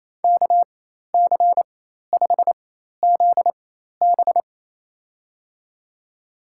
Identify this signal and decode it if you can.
Morse code, heard as a keyed tone, about 28 wpm, spaced as for 13 wpm, 700 Hz. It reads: KC5ZB
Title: USA Call Signs - 4x Character Spacing - 28wpm